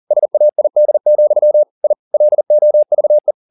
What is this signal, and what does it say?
Morse code, heard as a keyed tone, about 40 wpm, 600 Hz. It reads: SAID, I LOVE